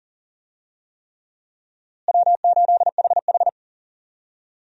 Morse code, heard as a keyed tone, about 39 wpm, 700 Hz. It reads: W8HH
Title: USA Call Signs - 4 Characters - Repeated 3x - 40wpm